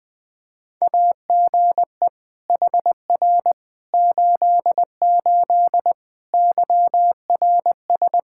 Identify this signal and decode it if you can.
Morse code, heard as a keyed tone, about 21 wpm, 700 Hz. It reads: AGE HR 88 YRS
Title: Practice QSO - Exchange 3 - More Info - 20wpm